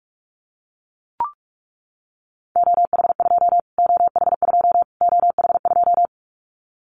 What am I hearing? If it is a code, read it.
Morse code, heard as a keyed tone, about 44 wpm, 700 Hz. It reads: OH2 OH2 OH2
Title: Callsign Prefixes for Top 25 DXCC - Repeated 3x - 45wpm